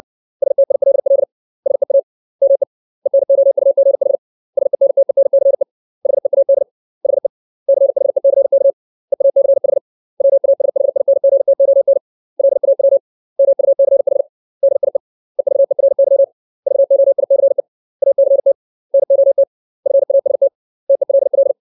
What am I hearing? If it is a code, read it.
Morse code, heard as a keyed tone, about 60 wpm, 550 Hz. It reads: STILL SEA ME ENOUGH SENTENCE HEAD HE CHECK EACH WASHINGTON LAW MUCH DIE EVERY VOICE ACT NOT FAST TELL
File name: 632 words / 632 Words 60WPM